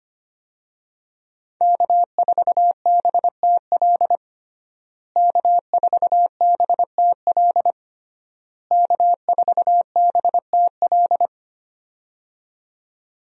Morse code, heard as a keyed tone, about 25 wpm, 700 Hz. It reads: K4BTL K4BTL K4BTL
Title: USA Call Signs - 5 Characters - Repeated 3x - 25wpm